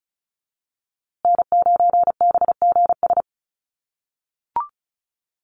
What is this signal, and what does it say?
Morse code, heard as a keyed tone, about 34 wpm, 700 Hz. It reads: N9BGS
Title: USA Call Signs Repeated 3x 35wpm